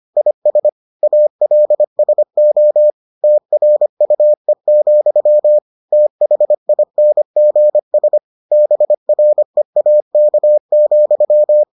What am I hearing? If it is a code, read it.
Morse code, heard as a keyed tone, about 25 wpm, 600 Hz. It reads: IS ALSO TRUE, THINGS BREAK,